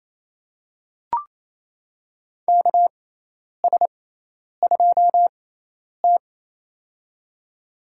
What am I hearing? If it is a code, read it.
Morse code, heard as a keyed tone, about 27 wpm, spaced as for 10 wpm, 700 Hz. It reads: KS2T
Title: USA Call Signs - 6x Character Spacing - 28wpm